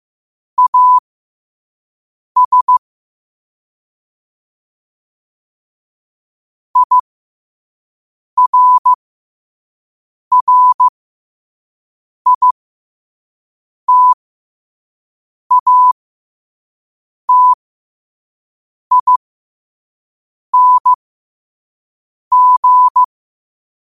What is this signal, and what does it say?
Morse code, heard as a keyed tone, about 15 wpm, spaced as for 5 wpm, 1000 Hz. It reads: AS IRRITATING
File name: qotdfiles / 5wpm